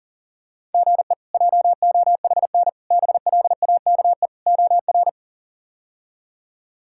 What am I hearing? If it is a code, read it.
Morse code, heard as a keyed tone, about 40 wpm, 700 Hz. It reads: GE JOHN BLAKE OR
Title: K1USN SST QSO Practice - 40wpm